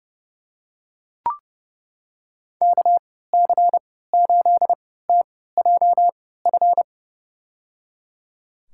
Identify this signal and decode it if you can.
Morse code, heard as a keyed tone, about 30 wpm, spaced as for 17 wpm, 700 Hz. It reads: KC8TJF
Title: USA Call Signs - 6 Characters - 3x Character Spacing - 30wpm